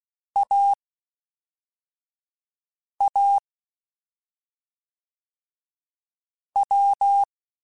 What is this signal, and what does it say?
Morse code, heard as a keyed tone, about 16 wpm, 800 Hz. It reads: A A W
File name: Difficult Combos / Difficult Combos-mp3s / awj1 ndb6 gz7 @5